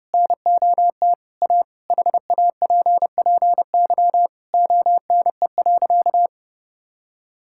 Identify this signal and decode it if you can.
Morse code, heard as a keyed tone, about 32 wpm, 700 Hz. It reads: NOT A HAPPY ONE.
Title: Sentences from Top 1000 Words - Review Part 5 - 30wpm